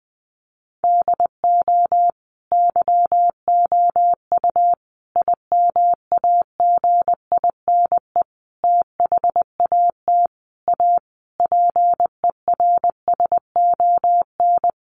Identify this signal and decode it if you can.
Morse code, heard as a keyed tone, about 20 wpm, 700 Hz. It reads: DO YOU IMAGINE THAT A PERSON
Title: Sentences from Top 1400 Words 20wpm